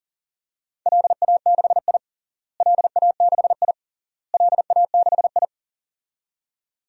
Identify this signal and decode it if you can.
Morse code, heard as a keyed tone, about 37 wpm, 700 Hz. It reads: LA6I LA6I LA6I
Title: DXCC-100 Call Signs - 4 Characters - Repeated 3x - 40wpm